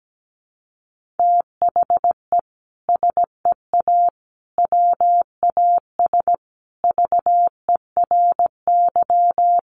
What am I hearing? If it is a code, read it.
Morse code, heard as a keyed tone, about 17 wpm, 700 Hz. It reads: THE SEA WAS VERY
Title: Sentences from Top 1400 Words 17wpm